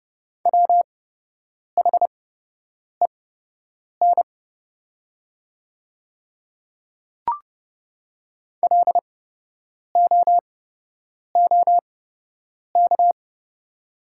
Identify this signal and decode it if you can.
Morse code, heard as a keyed tone, about 30 wpm, spaced as for 8 wpm, 700 Hz. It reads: WHEN LOOK
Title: 8x Character Spacing - Top 200 Words - 30wpm